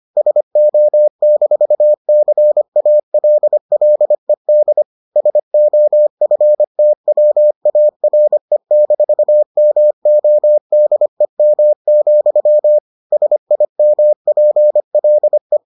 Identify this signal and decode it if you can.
Morse code, heard as a keyed tone, about 25 wpm, 600 Hz. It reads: SO-CALLED SOFTWARE-MODEM, SIMPLE